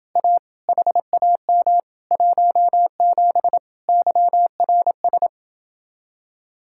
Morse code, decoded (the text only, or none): A HAM 17 YRS